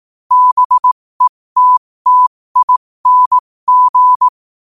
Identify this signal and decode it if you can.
Morse code, heard as a keyed tone, about 18 wpm, spaced as for 15 wpm, 1000 Hz. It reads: BETTING
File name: qotdfiles / 15wpm